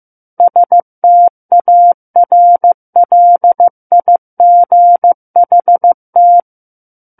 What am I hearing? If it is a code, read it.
Morse code, heard as a keyed tone, about 15 wpm, 700 Hz. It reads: STARLIGHT
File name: Book One / 06-Chapter 6 - 15 WPM